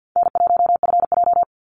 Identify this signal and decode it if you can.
Morse code, heard as a keyed tone, about 50 wpm, 700 Hz. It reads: N1FJ
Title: NA SOTA Chaser Callsigns 50wpm